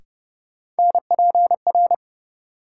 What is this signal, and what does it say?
Morse code, heard as a keyed tone, about 29 wpm, 700 Hz. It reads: NPR
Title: Sets of 5 Words - New Word Drill - 30wpm